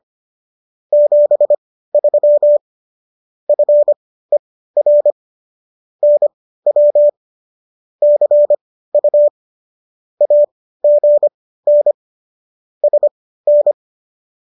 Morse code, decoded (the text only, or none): 73 FER NW CU AGN SN